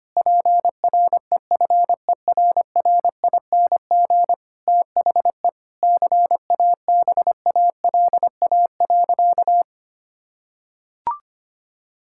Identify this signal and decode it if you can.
Morse code, heard as a keyed tone, about 25 wpm, 700 Hz. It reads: PREFERRING THE CABALA.
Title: Pangram Challenge v2 - 25wpm